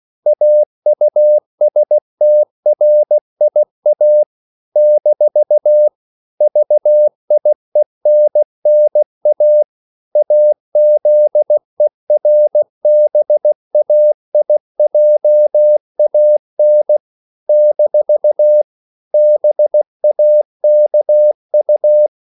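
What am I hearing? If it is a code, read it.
Morse code, heard as a keyed tone, about 16 wpm, 600 Hz. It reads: AUSTRIA - VIENNA AZERBAIJAN - BAKU
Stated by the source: CW practice of national capitals at 16 words per minute, from ON6ZQ.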